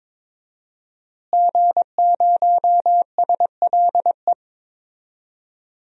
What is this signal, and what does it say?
Morse code, heard as a keyed tone, about 22 wpm, 700 Hz. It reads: G0SLE
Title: DXCC-100 Call Signs - 5 Characters - Repeated 3x - 22wpm